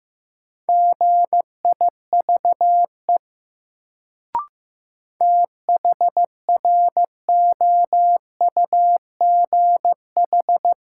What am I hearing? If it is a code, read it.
Morse code, heard as a keyed tone, about 15 wpm, 700 Hz. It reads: GIVE THROUGH
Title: Sentences from Top 200 Words - New Word Drill - 15wpm